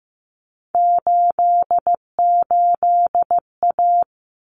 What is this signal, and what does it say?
Morse code, heard as a keyed tone, about 15 wpm, 700 Hz. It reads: 88A
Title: Common Cut Numbers - 3-Digit 15wpm